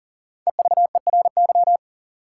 Morse code, heard as a keyed tone, about 40 wpm, 700 Hz. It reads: EVERY
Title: Sentences from Top 200 Words - New Word Drill - 40wpm